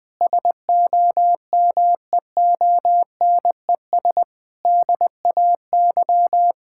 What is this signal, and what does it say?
Morse code, heard as a keyed tone, about 20 wpm, 700 Hz. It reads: SOMEONES DAY
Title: Difficult Sentences with Hints - 20wpm